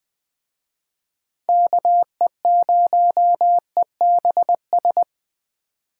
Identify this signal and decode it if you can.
Morse code, heard as a keyed tone, about 20 wpm, 700 Hz. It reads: KE0EBS